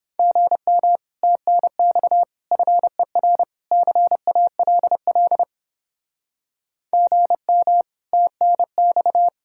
Morse code, heard as a keyed tone, about 29 wpm, 700 Hz. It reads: GM TNX FER CALL GM TNX